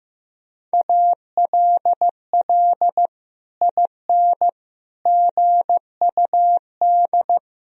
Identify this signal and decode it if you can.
Morse code, heard as a keyed tone, about 15 wpm, 700 Hz. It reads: ALL IN GUD
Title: Sentences from Top 200 Words and Abbreviations 15wpm